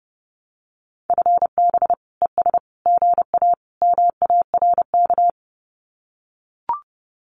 Morse code, heard as a keyed tone, about 30 wpm, 700 Hz. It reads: FB ES GA MARK
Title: Common QSO Phrases Encore 30wpm